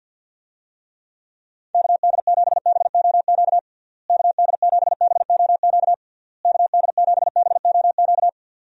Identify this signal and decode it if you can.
Morse code, heard as a keyed tone, about 51 wpm, 700 Hz. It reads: KD7BOQ KD7BOQ KD7BOQ
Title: USA Call Signs - 6 Characters - Repeated 3x - 50wpm